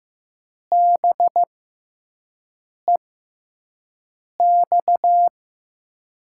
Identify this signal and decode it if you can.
Morse code, heard as a keyed tone, about 15 wpm, spaced as for 5 wpm, 700 Hz. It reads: BEX
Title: USA Call Signs - 6x Character Spacing - 15wpm